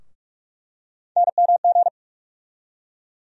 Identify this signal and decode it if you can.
Morse code, heard as a keyed tone, about 46 wpm, 700 Hz. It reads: NMG
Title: USA Call Sign Postfixes - 3 Letters 45wpm